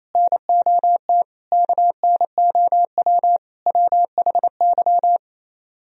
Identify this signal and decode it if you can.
Morse code, heard as a keyed tone, about 36 wpm, 690 Hz. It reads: NOT KNOW WHY